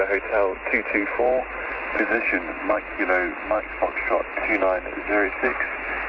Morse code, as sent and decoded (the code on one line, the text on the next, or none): none